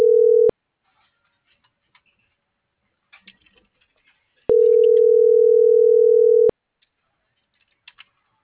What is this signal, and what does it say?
Ringback tone (North America), 440 + 480 Hz.